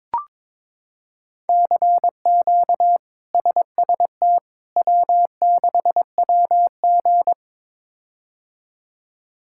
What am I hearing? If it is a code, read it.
Morse code, heard as a keyed tone, about 22 wpm, 700 Hz. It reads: CQ SST W6WG
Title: K1USN SST QSO Practice - 22wpm